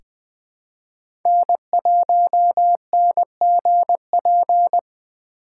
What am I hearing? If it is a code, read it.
Morse code, heard as a keyed tone, about 20 wpm, 700 Hz. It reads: N1NGP
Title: USA Call Signs - 5 Characters - 3x Character Spacing - 20wpm